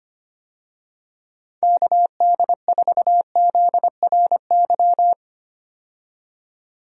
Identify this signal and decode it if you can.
Morse code, heard as a keyed tone, about 25 wpm, 700 Hz. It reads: KD4ZRY
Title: USA Call Signs - 6 Characters - 25wpm